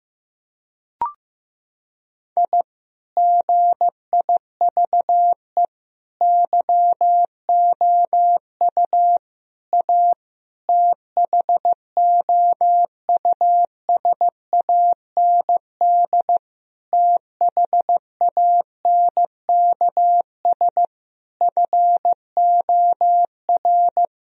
Morse code, heard as a keyed tone, about 15 wpm, 700 Hz. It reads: I GIVE YOU A THOUSAND THANKS FOR
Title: Sentences from Top 1000 Words - Review Part 4 - 15wpm